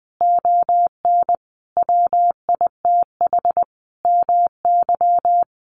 Morse code, heard as a keyed tone, about 19 wpm, 700 Hz. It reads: ON WITH MY
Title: Sentences from Top 300 Words 20wpm